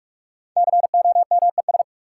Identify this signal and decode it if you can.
Morse code, heard as a keyed tone, about 46 wpm, 700 Hz. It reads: COMES
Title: Sentences from Top 1000 Words - Review Part 5 - New Word Drill - 45wpm